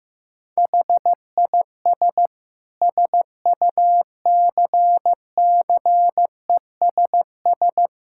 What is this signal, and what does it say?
Morse code, heard as a keyed tone, about 15 wpm, 700 Hz. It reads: HIS SUCCESS
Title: Sentences from Top 1000 Words - Review Part 6 - 15wpm